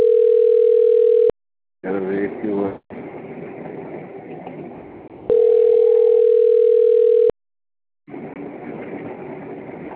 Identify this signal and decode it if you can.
Ringback tone (North America), 440 + 480 Hz.
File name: out- - - - - .86146